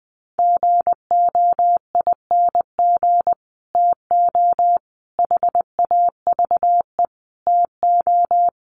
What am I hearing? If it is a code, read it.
Morse code, heard as a keyed tone, about 20 wpm, 700 Hz. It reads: GOING TO HAVE TO
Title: Sets of 5 Words Encore 20wpm